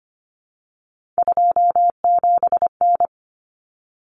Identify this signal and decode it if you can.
Morse code, heard as a keyed tone, about 25 wpm, 700 Hz. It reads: 27N
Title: Common Cut Numbers - 3-Digit 25wpm